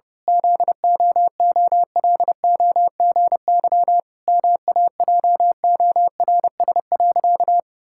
Morse code, heard as a keyed tone, about 28 wpm, 700 Hz. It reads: ZOOLOGY MAJORS.
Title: Pangram Challenge v2 - 30wpm